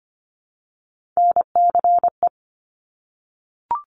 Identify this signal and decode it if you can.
Morse code, heard as a keyed tone, about 25 wpm, 700 Hz. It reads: NCE
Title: Top Three Letter Combinations in Words 25wpm